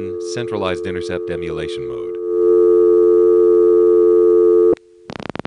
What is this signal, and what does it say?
Dial tone (North America), 350 + 440 Hz, with speech.
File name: phonetrips / SmyrnaGA5HQ